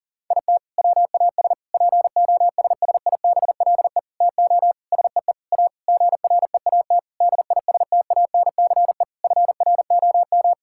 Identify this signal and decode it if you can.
Morse code, heard as a keyed tone, about 40 wpm, 700 Hz. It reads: IT WAS POSSIBLE TO SEE A GREAT DISTANCE FROM
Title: Sentences from Top 1600 Words 40wpm